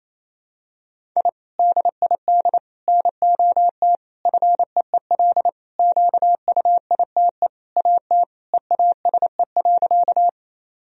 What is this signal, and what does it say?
Morse code, heard as a keyed tone, about 28 wpm, 700 Hz. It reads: I DID NOT FEEL QUITE AT EASE.
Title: Sentences from Top 1000 Words - Review Part 1 - 28wpm